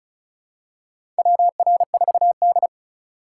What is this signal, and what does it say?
Morse code, heard as a keyed tone, about 36 wpm, 700 Hz. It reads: WR4D
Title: USA Call Signs - 4 Characters - 3x Character Spacing - 35wpm